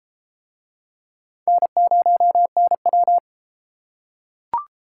Morse code, heard as a keyed tone, about 33 wpm, 700 Hz. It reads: N0NW